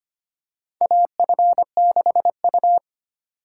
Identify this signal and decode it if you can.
Morse code, heard as a keyed tone, about 25 wpm, 700 Hz. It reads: AF6U
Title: USA Call Signs - 6x Character Spacing - 25wpm